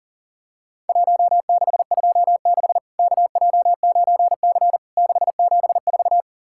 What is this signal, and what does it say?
Morse code, heard as a keyed tone, about 43 wpm, 700 Hz. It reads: 1626 KJ9C 674
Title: Worked All Europe DX Contest - QTC Exchange - 40wpm